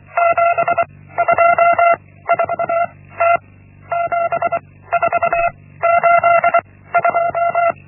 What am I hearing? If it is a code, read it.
Morse code, heard as a keyed tone, about 24 wpm, 680 Hz. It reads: 7 2 4 T 7 4 8 2
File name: Linksmakalnis-1